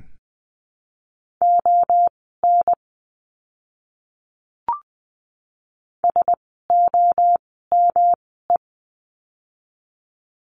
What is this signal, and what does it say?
Morse code, heard as a keyed tone, about 19 wpm, spaced as for 14 wpm, 700 Hz. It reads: ON SOME
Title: 160 Lesson 26f - Letter g - Words 20wpm